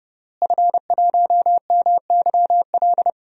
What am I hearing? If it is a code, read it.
Morse code, heard as a keyed tone, about 30 wpm, 700 Hz. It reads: F1MYL
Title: DXCC-100 Call Signs - 5 Characters - Repeated 3x - 30wpm